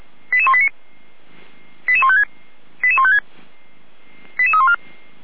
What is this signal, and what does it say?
Five-tone selective call (ZVEI), heard as quick runs of five tones, tones of about 70 ms. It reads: 88189 88156 88156 88314
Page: SDR/ZVEI